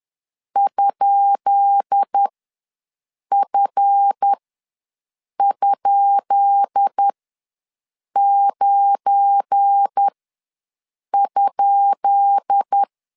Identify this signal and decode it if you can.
Morse code, heard as a keyed tone, about 10 wpm, spaced as for 6 wpm, 800 Hz. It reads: ?F?9?
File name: LEZIONI / 10 SIGNS / 10B INT ws 10wpm 4134